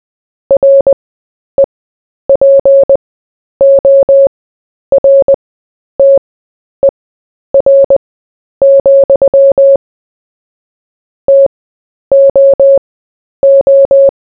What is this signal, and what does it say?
Morse code, heard as a keyed tone, about 20 wpm, spaced as for 10 wpm, 550 Hz. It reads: REPORTER, TOO
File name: cw / fists / Short-Story-4-10wpm